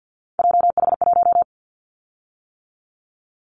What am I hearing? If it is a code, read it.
Morse code, heard as a keyed tone, about 50 wpm, 700 Hz. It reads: JH1
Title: Callsign Prefixes for Top 25 DXCC - 50wpm